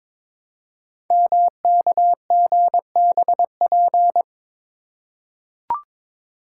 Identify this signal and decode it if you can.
Morse code, heard as a keyed tone, about 23 wpm, 700 Hz. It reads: MKGBP
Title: Code Groups - 5 Letters - 22wpm